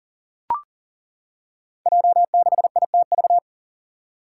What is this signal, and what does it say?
Morse code, heard as a keyed tone, about 40 wpm, 700 Hz. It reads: J6ITV
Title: Five Challenging Letter-Number Combinations 40wpm